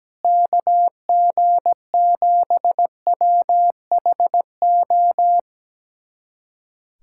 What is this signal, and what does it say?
Morse code, heard as a keyed tone, about 17 wpm, 700 Hz. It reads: KG7WHO
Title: USA Callsign DE Callsign 17wpm